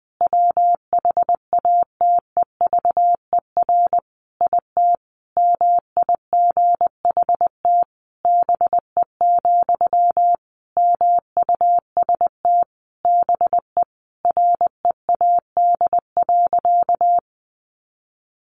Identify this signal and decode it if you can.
Morse code, heard as a keyed tone, about 20 wpm, 700 Hz. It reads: WHATEVER IT MIGHT BE, MUST BE READ.
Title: Sentences from Top 1100 Words 20wpm